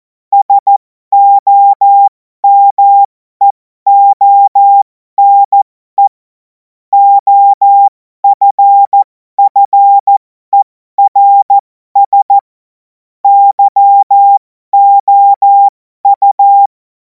Morse code, heard as a keyed tone, about 14 wpm, spaced as for 12 wpm, 800 Hz. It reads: SOMEONE OFFERS YOU
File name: Morsekurs / Witze